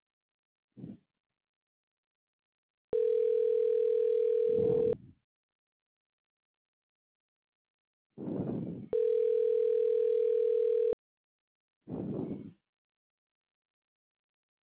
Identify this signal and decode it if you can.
Ringback tone (North America), 440 + 480 Hz.